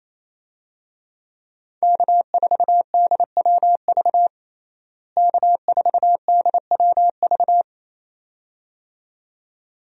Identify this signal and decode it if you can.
Morse code, heard as a keyed tone, about 28 wpm, 700 Hz. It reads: K4DWV K4DWV
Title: USA Call Signs - 5 Characters - Repeated 2x - 28wpm